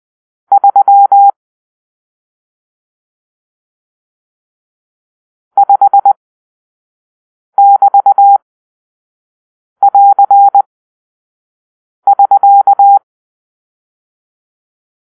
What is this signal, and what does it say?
Morse code, heard as a keyed tone, about 20 wpm, spaced as for 5 wpm, 800 Hz. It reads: 3 5<BT><AR><SK>
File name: arrl / Grp-6 Drill-1